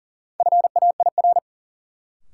Morse code, heard as a keyed tone, about 40 wpm, 700 Hz. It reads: FAIR
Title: Sentences from Top 1300 Words - New Word Drill Alt - 40wpm